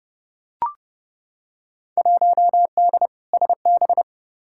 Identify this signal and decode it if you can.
Morse code, heard as a keyed tone, about 32 wpm, 700 Hz. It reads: 1D SB
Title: Field Day Exchanges - 30wpm